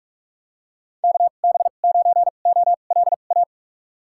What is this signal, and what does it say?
Morse code, heard as a keyed tone, about 46 wpm, spaced as for 32 wpm, 700 Hz. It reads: KB9OLA
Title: USA Call Signs - 6 Characters - 2x Character Spacing - 45wpm